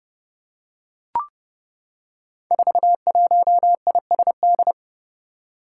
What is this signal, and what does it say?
Morse code, heard as a keyed tone, about 31 wpm, 700 Hz. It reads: 41ISD
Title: Five Challenging Letter-Number Combinations 30wpm